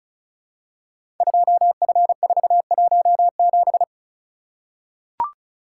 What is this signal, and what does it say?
Morse code, heard as a keyed tone, about 36 wpm, 690 Hz. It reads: 2F417
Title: Code Groups - 5 Letter:Numbers - 35wpm